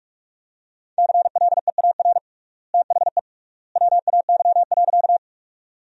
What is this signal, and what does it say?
Morse code, heard as a keyed tone, about 47 wpm, 700 Hz. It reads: CLEAR THE WAY.
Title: 3x Word-Spacing - Sentences from Top 500 Words - 45wpm